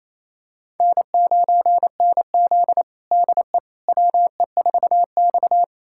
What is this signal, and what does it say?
Morse code, heard as a keyed tone, about 28 wpm, 700 Hz. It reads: N9NZ DE WE4X
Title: USA Callsign DE Callsign - 4 Characters - 28wpm